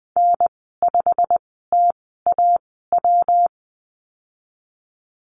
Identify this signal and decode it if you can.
Morse code, heard as a keyed tone, about 19 wpm, spaced as for 14 wpm, 700 Hz. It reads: N5TAW
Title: 167 Lesson 27g - Number 7 - Call Signs 20wpm